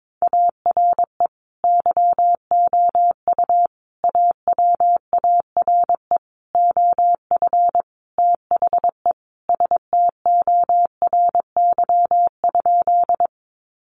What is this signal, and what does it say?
Morse code, heard as a keyed tone, about 22 wpm, 700 Hz. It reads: ARE YOU AWARE OF THE STORY?
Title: Sentences from Top 1200 Words 22wpm